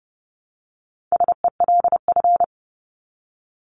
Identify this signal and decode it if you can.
Morse code, heard as a keyed tone, about 30 wpm, 700 Hz. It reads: SELF
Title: Top 500 Words 30wpm